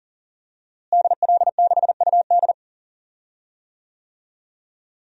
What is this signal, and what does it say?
Morse code, heard as a keyed tone, about 37 wpm, 700 Hz. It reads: DL6UD
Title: DXCC-100 Call Signs - 5 Characters - Repeated 3x - 40wpm